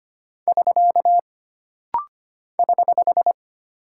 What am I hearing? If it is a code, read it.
Morse code, heard as a keyed tone, about 25 wpm, 700 Hz. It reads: <SK> <HH>
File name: Single Punctuation-Prosign - Rapid-Fire - 25wpm